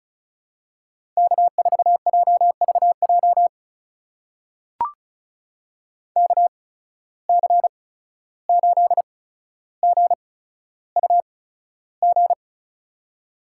Morse code, heard as a keyed tone, about 36 wpm, spaced as for 14 wpm, 700 Hz. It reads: K4JVJ KC8GUG
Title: USA Call Signs - 8x Character Spacing - 35wpm